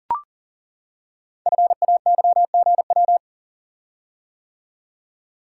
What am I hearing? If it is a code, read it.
Morse code, heard as a keyed tone, about 39 wpm, 690 Hz. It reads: FAYGW
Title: Code Groups - 5 Letters - 40wpm